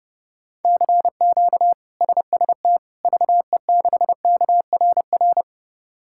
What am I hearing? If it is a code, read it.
Morse code, heard as a keyed tone, about 30 wpm, 700 Hz. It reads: CQ SST VE6KRR
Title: K1USN SST QSO Practice - 30wpm